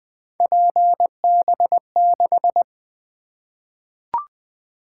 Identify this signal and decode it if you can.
Morse code, heard as a keyed tone, about 20 wpm, 700 Hz. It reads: PB6
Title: Three Letter-Number Combinations - 20wpm